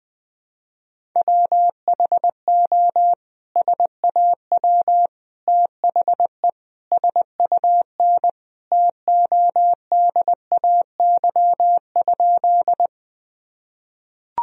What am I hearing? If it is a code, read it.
Morse code, heard as a keyed tone, about 21 wpm, 700 Hz. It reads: WHO SAW THE SUN TODAY?
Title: Sentences from Top 1000 Words - Review Part 1 - 20wpm